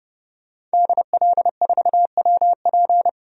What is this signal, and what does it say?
Morse code, heard as a keyed tone, about 29 wpm, 700 Hz. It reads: DL4WP
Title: DXCC-100 Call Signs - 5 Characters - Repeated 2x - 30wpm